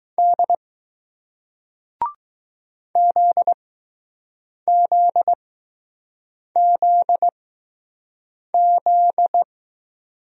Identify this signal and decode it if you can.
Morse code, heard as a keyed tone, about 20 wpm, spaced as for 6 wpm, 700 Hz. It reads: D ZZZZ